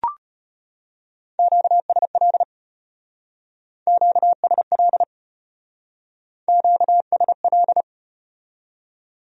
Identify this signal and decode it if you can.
Morse code, heard as a keyed tone, about 33 wpm, 700 Hz. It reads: QSL QSL QSL